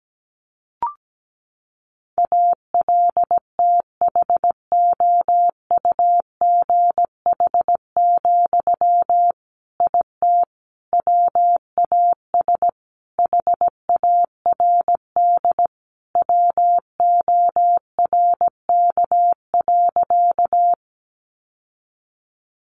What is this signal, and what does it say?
Morse code, heard as a keyed tone, about 17 wpm, 700 Hz. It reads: ALTHOUGH, IT WAS HARD WORK.
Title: Sentences from Top 600 Words 17wpm